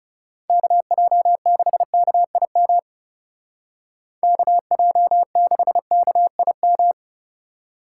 Morse code, heard as a keyed tone, about 34 wpm, 700 Hz. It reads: KJ6KIM KJ6KIM
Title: Speed Racing - USA 2x3 Call Signs 30wpm